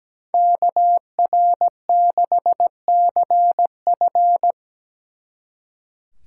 Morse code, heard as a keyed tone, about 20 wpm, 700 Hz. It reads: KR6CF
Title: Speed Racing - USA 2x2 Call Signs 17wpm